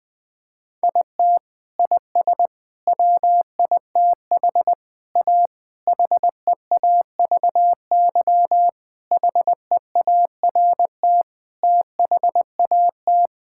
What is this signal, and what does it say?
Morse code, heard as a keyed tone, about 20 wpm, 700 Hz. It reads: IT IS WITH A HEAVY HEART THAT
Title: Sets of 8 Words - 20wpm